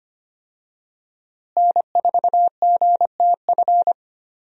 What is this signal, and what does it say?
Morse code, heard as a keyed tone, about 25 wpm, 700 Hz. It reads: N4GTF
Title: USA Call Signs - 5 Characters - Repeated 2x - 25wpm